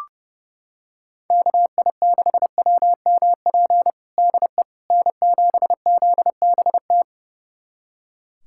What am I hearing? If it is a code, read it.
Morse code, heard as a keyed tone, about 30 wpm, 700 Hz. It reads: KI6WMP DE N7ZBT
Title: USA Callsign DE Callsign 30wpm